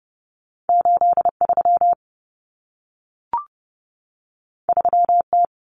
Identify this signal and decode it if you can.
Morse code, heard as a keyed tone, about 30 wpm, 700 Hz. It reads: 83 3T
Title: Common Cut Numbers - 2-Digit 30wpm